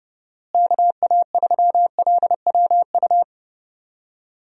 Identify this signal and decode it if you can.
Morse code, heard as a keyed tone, about 32 wpm, 700 Hz. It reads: KA3LWU
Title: USA Call Signs - 6 Characters - Repeated 2x - 30wpm